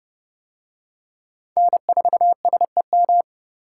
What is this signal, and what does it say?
Morse code, heard as a keyed tone, about 29 wpm, 700 Hz. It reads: N4SEM